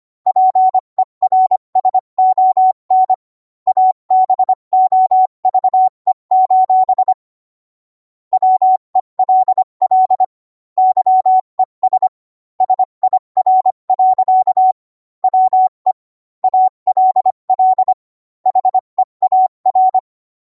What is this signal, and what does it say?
Morse code, heard as a keyed tone, about 25 wpm, 750 Hz. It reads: PERSON ABOVE: WELL YES SIR. WE ALL HEAR